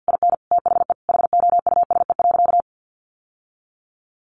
Morse code, heard as a keyed tone, about 50 wpm, 700 Hz. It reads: IN THE HOUSE.
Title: Sentences from Top 1300 Words 50wpm